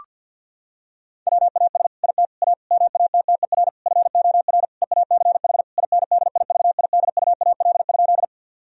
Morse code, heard as a keyed tone, about 50 wpm, 700 Hz. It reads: WAS IT A MATTER FOR EACH INDIVIDUAL?
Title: Sentences from Top 1000 Words - Review Part 4 - 50wpm